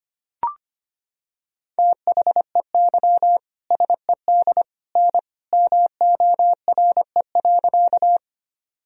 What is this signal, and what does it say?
Morse code, heard as a keyed tone, about 25 wpm, 700 Hz. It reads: THEY SED N MORE.
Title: Sentences from Top 100 Words and Abbreviations 25wpm